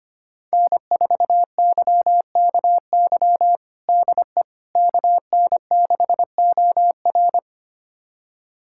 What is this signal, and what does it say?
Morse code, heard as a keyed tone, about 25 wpm, 700 Hz. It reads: N4YKY DE KN6OR
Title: USA Callsign DE Callsign 25wpm